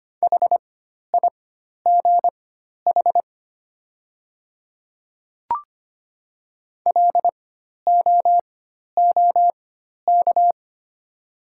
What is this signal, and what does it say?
Morse code, heard as a keyed tone, about 25 wpm, spaced as for 12 wpm, 700 Hz. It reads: HIGH LOOK
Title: 4x Character Spacing - Top 500 Words - 25wpm